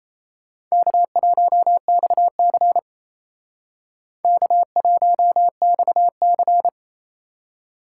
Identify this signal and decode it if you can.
Morse code, heard as a keyed tone, about 35 wpm, 700 Hz. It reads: K1XC K1XC